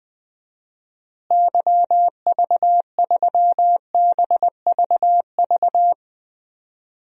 Morse code, heard as a keyed tone, about 19 wpm, 700 Hz. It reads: YV3BVV
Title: DXCC-100 Call Signs - 6 Characters - Repeated 3x - 20wpm